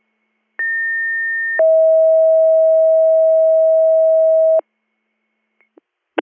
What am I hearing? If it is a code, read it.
Two-tone page, heard as a one-second tone, then a three-second tone, about 1800 then 640 Hz. It is heard from 0.6 to 4.6 s.